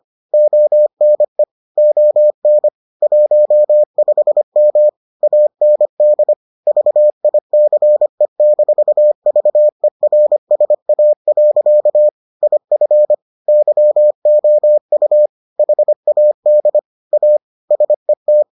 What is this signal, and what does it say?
Morse code, heard as a keyed tone, about 25 wpm, 600 Hz. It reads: ONE ON 15M AND VICE-VERSA. IF YOU HAD A SET